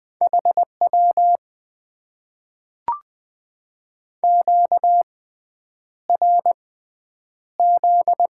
Morse code, heard as a keyed tone, about 20 wpm, spaced as for 9 wpm, 700 Hz. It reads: HW QRZ
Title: Top 90 QSO Elements - 6x Character Spacing - 20wpm